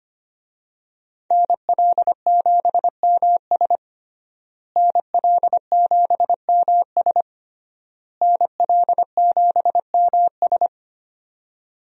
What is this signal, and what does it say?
Morse code, heard as a keyed tone, about 25 wpm, 700 Hz. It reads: NL7MS NL7MS NL7MS
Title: USA Call Signs - 5 Characters - Repeated 3x - 25wpm